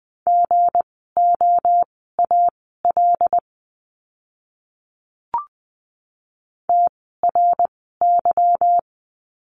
Morse code, heard as a keyed tone, about 20 wpm, spaced as for 15 wpm, 700 Hz. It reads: GOAL TRY
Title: 160 Lesson 26f - Letter g - Words 20wpm